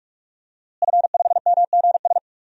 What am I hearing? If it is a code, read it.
Morse code, heard as a keyed tone, about 43 wpm, 700 Hz. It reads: F5MGS
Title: DXCC-100 Call Signs - 5 Characters - Repeated 3x - 45wpm